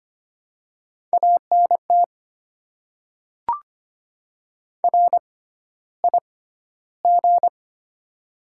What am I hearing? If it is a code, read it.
Morse code, heard as a keyed tone, about 25 wpm, spaced as for 13 wpm, 700 Hz. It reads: ANT RIG
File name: Top 20 QSO Elements - 6x Character Spacing - 25wpm